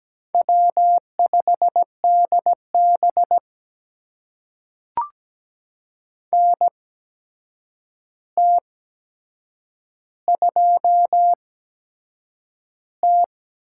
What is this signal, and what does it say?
Morse code, heard as a keyed tone, about 17 wpm, spaced as for 7 wpm, 700 Hz. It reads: W5DB NT2T
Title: USA Call Signs - 4 Characters - 8x Character Spacing - 17wpm